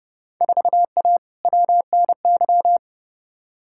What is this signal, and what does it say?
Morse code, heard as a keyed tone, about 29 wpm, 700 Hz. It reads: 4A WNY
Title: Field Day Exchanges Encore - 30wpm